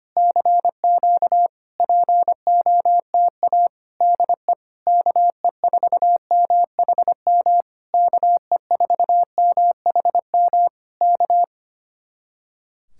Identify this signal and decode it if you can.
Morse code, heard as a keyed tone, about 27 wpm, 700 Hz. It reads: CQ POTA DE KE4MHM KE4MHM K
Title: Speed-Racing Common QSO Phrases Encore - 25wpm